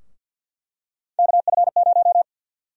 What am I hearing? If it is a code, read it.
Morse code, heard as a keyed tone, about 48 wpm, 700 Hz. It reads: KF0
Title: USA Call Sign Prefixes - Repeated 2x - 50wpm